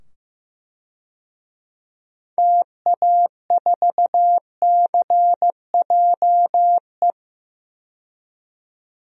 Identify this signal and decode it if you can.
Morse code, heard as a keyed tone, about 15 wpm, 700 Hz. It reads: TA4CJE
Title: DXCC-100 Call Signs - 6 Characters - 15wpm